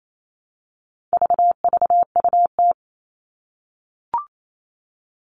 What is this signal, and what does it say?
Morse code, heard as a keyed tone, about 28 wpm, 700 Hz. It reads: VVUT
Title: All Cut Numbers - 4-Digit 28wpm